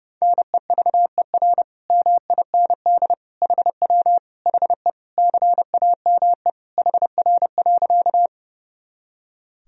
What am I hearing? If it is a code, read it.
Morse code, heard as a keyed tone, about 32 wpm, 700 Hz. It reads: NEVER MIND HW HE CAME HR.
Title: Sentences from Top 500 Words and Abbreviations 30wpm